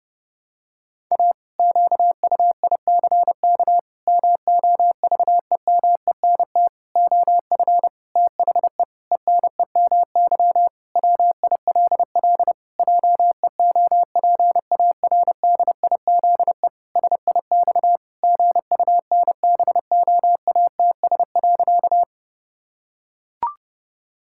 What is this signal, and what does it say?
Morse code, heard as a keyed tone, about 32 wpm, 700 Hz. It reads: A QUICK MOVEMENT OF THE ENEMY WILL JEOPARDIZE SIX GUNBOATS.
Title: Pangram Challenge v2 - 30wpm